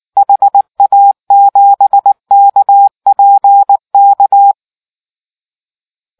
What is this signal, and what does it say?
Morse code, heard as a keyed tone, about 19 wpm, 800 Hz. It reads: HA7KPK